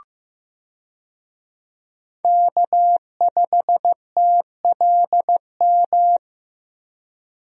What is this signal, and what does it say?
Morse code, heard as a keyed tone, about 15 wpm, 700 Hz. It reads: K5TLM
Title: USA Call Signs - 5 Characters - Repeated 3x - 15wpm